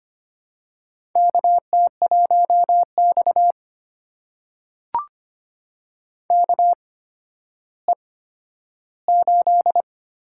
Morse code, heard as a keyed tone, about 25 wpm, spaced as for 12 wpm, 700 Hz. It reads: KT1X KE8
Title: USA Call Signs - 4 Characters - 8x Character Spacing - 25wpm